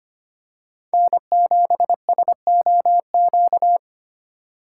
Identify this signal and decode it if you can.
Morse code, heard as a keyed tone, about 25 wpm, 700 Hz. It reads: N7SOQ
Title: USA Call Signs - 8x Character Spacing - 25wpm